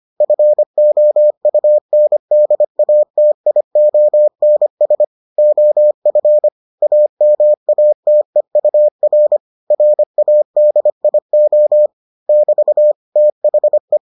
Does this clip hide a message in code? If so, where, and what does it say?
Yes: Morse code from 0.2 to 14.0 s, heard as a keyed tone, about 25 wpm, 600 Hz. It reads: FOUNDATIONS OF AMATEUR RADIO <BT> THE